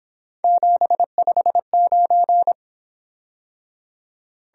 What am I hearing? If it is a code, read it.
Morse code, heard as a keyed tone, about 26 wpm, 700 Hz. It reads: 759
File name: Speed-Racing - Three Letter-Number Combinations - 22wpm